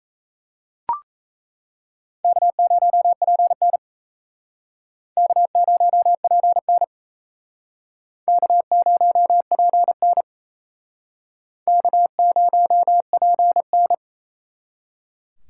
Morse code, heard as a keyed tone, about 36 wpm, 700 Hz. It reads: K0PN K0PN K0PN K0PN